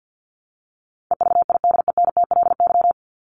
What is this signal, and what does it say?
Morse code, heard as a keyed tone, about 50 wpm, 700 Hz. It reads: EVIDENTLY
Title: LY Suffix - Top 100 Words with Base Words 50wpm